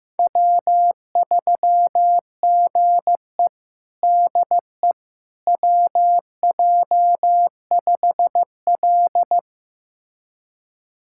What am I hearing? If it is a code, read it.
Morse code, heard as a keyed tone, about 15 wpm, 700 Hz. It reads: W3GE DE WJ5L
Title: USA Callsign DE Callsign - 4 Characters - 15wpm